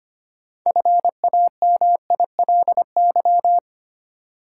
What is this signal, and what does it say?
Morse code, heard as a keyed tone, about 25 wpm, 700 Hz. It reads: FAMILY
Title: Sentences from Top 400 Words - New Word Drill - 25wpm